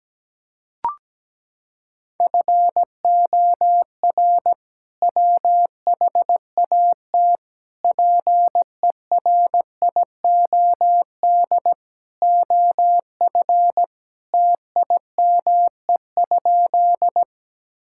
Morse code, heard as a keyed tone, about 17 wpm, 700 Hz. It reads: FOR WHAT PERIOD OF TIME?
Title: Sentences from Top 1000 Words - Review Part 5 - 17wpm